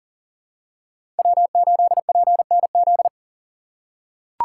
Morse code, heard as a keyed tone, about 37 wpm, 700 Hz. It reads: W8PNZ
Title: USA Call Signs - 2x Character Spacing - 40wpm